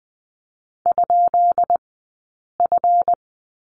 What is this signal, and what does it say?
Morse code, heard as a keyed tone, about 20 wpm, spaced as for 8 wpm, 700 Hz. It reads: ?F
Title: 197 Lesson 32d - Letter k - Two Characters 20wpm